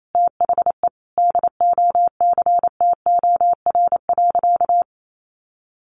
Morse code, heard as a keyed tone, about 28 wpm, 700 Hz. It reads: THE DOCTOR.
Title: Sentences from Top 1500 Words 28wpm